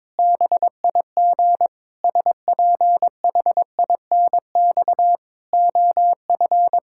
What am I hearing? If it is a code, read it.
Morse code, heard as a keyed tone, about 22 wpm, 700 Hz. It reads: BIG SPHINX OF
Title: Pangram Challenge v2 - 22wpm